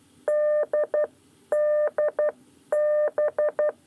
Morse code, heard as a keyed tone, about 12 wpm, spaced as for 10 wpm, 600 Hz. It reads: D DB
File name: 9 lezione cw 2